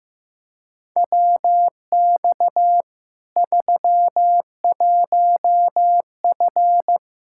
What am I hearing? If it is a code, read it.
Morse code, heard as a keyed tone, about 15 wpm, 700 Hz. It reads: WX 31F
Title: Practice QSO - Exchange 2 - Info - 15wpm